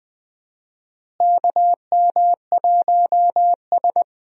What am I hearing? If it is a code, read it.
Morse code, heard as a keyed tone, about 20 wpm, 700 Hz. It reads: KM1S
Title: USA Call Signs - 4 Characters - Repeated 3x - 20wpm